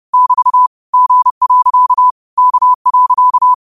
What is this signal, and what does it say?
Morse code, heard as a keyed tone, about 30 wpm, 1000 Hz. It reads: <BT> G. K.